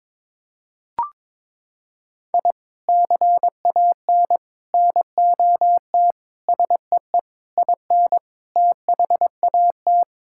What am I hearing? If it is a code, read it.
Morse code, heard as a keyed tone, about 22 wpm, 700 Hz. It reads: I CAN NOT SEE IN THAT
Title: Sentences from Top 1000 Words - Review Part 5 - 22wpm